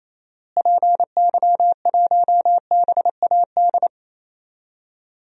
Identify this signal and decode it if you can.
Morse code, heard as a keyed tone, about 28 wpm, 700 Hz. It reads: PY1BAD
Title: DXCC-100 Call Signs - 6 Characters - Repeated 3x - 28wpm